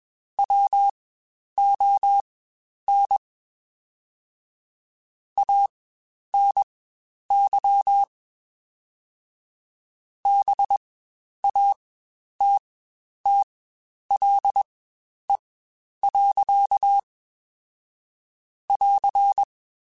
Morse code, heard as a keyed tone, about 21 wpm, spaced as for 10 wpm, 800 Hz. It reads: WON ANY BATTLE. <AR>
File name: Quotes Spaced / Quotes Spaced-mp3s / Pessimism sp @20